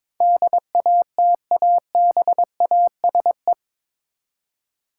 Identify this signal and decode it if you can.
Morse code, heard as a keyed tone, about 22 wpm, 700 Hz. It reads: DATABASE
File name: practice / 22wpm / Eight Letter Words - 22wpm